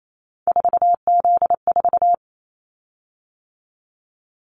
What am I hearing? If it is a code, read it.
Morse code, heard as a keyed tone, about 28 wpm, 700 Hz. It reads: 4Z4
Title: Callsign Prefixes for Top 50 DXCC - 28wpm